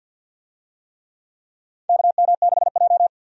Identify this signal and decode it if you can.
Morse code, heard as a keyed tone, about 51 wpm, 700 Hz. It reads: KM6J
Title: USA Call Signs - 4 Characters - 50wpm